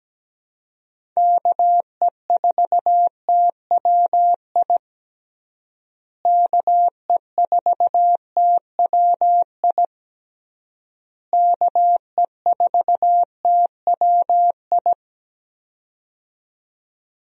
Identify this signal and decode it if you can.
Morse code, heard as a keyed tone, about 17 wpm, 700 Hz. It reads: KE4TWI KE4TWI KE4TWI
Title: USA Call Signs - 6 Characters - Repeated 3x - 17wpm